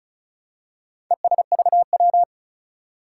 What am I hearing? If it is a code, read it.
Morse code, heard as a keyed tone, about 36 wpm, 700 Hz. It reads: ESVW
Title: Four Challenging Letter-Number Combinations 35wpm